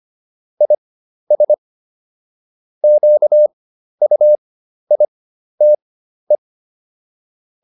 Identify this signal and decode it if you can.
Morse code, heard as a keyed tone, about 25 wpm, spaced as for 12 wpm, 600 Hz. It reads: IS QUITE